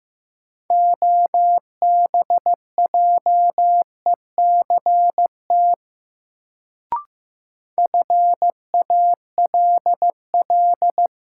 Morse code, heard as a keyed tone, about 15 wpm, 700 Hz. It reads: OBJECT FALL
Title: Sentences from Top 500 Words - New Word Drill - 15wpm